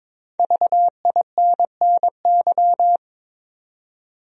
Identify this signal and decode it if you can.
Morse code, heard as a keyed tone, about 22 wpm, 700 Hz. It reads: VINNY
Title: USA Common Names - 22wpm